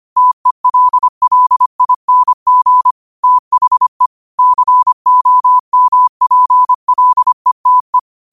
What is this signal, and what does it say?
Morse code, heard as a keyed tone, about 25 wpm, 1000 Hz. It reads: TELLING THE COMPLETE